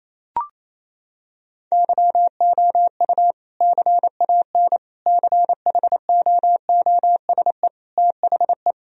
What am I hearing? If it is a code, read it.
Morse code, heard as a keyed tone, about 28 wpm, 700 Hz. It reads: YOU CAN CHOOSE THE
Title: Sentences from Top 1000 Words - Review Part 4 - 28wpm